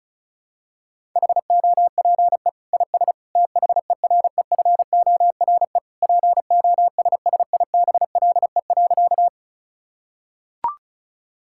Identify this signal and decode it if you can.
Morse code, heard as a keyed tone, about 36 wpm, 700 Hz. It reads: HOPE IS THEREFORE POSSIBLE.
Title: Sentences from Top 1000 Words - Review Part 5 - 35wpm